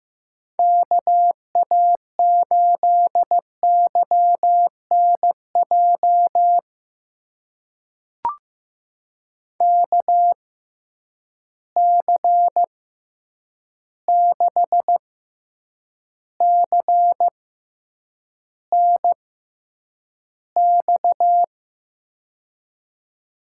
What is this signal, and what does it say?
Morse code, heard as a keyed tone, about 15 wpm, spaced as for 8 wpm, 700 Hz. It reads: KA8YNJ KC6CNX
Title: USA Call Signs - 6 Characters - 6x Character Spacing - 15wpm